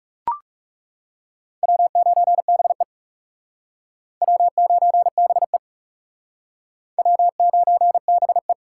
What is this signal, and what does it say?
Morse code, heard as a keyed tone, about 40 wpm, 700 Hz. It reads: W9BE W9BE W9BE